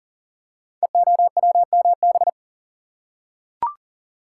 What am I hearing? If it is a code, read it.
Morse code, heard as a keyed tone, about 41 wpm, 700 Hz. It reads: EOWMB
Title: Code Groups - 5 Letter:Numbers - 40wpm